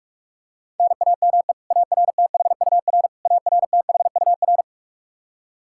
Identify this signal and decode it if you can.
Morse code, heard as a keyed tone, about 46 wpm, 700 Hz. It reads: NAME ARTHUR ARTHUR
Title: Practice QSO - Exchange 1 - Beginning - 45wpm